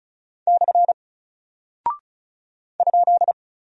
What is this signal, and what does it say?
Morse code, heard as a keyed tone, about 36 wpm, 700 Hz. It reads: / ?
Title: Single Character (Letter:Number:Punctuation:Prosign) Alt 35wpm